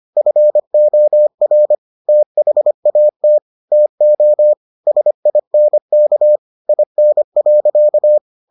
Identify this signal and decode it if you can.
Morse code, heard as a keyed tone, about 25 wpm, 600 Hz. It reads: FOR THAT TO SINK IN.